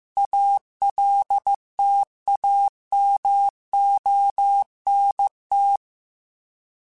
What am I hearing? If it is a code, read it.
Morse code, heard as a keyed tone, about 15 wpm, 800 Hz. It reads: ALTAMONT